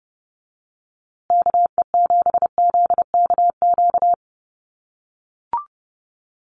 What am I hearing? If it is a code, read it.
Morse code, heard as a keyed tone, about 30 wpm, 700 Hz. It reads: KE7ZKQ
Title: USA Call Signs Repeated 2x 30wpm